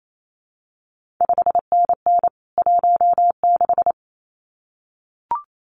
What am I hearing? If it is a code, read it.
Morse code, heard as a keyed tone, about 28 wpm, 700 Hz. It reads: 5NN 16
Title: CQ WW DX Contest Exchange - 28wpm